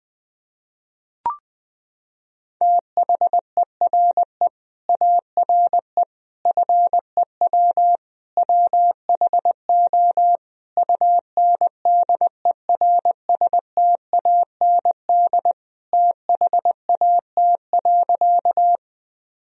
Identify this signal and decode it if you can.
Morse code, heard as a keyed tone, about 21 wpm, 700 Hz. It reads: THERE ARE FEW WHO UNDERSTAND THAT.
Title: Sentences from Top 1000 Words - Review Part 1 - 20wpm